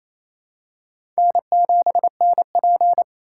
Morse code, heard as a keyed tone, about 27 wpm, 700 Hz. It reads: N7NP
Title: USA Call Signs - 4 Characters - 2x Character Spacing - 28wpm